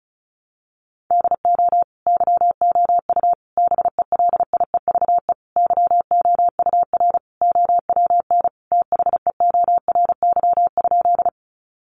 Morse code, heard as a keyed tone, about 34 wpm, 700 Hz. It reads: DO YOU BELIEVE YOUR OWN THEORY?
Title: Sentences from Top 1000 Words 35wpm